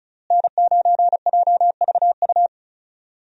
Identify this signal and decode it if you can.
Morse code, heard as a keyed tone, about 34 wpm, 700 Hz. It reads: N9JVU
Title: USA Call Signs - 5 Characters - Repeated 2x - 35wpm